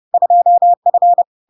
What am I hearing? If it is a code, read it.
Morse code, heard as a keyed tone, about 30 wpm, 700 Hz. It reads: 2F
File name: LARK-FD-2022-Log-in-CW-30wpm-700-Hz